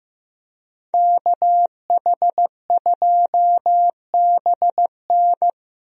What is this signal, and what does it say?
Morse code, heard as a keyed tone, about 15 wpm, 700 Hz. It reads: KH2BN
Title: USA Call Signs - 5 Characters - Repeated 2x - 15wpm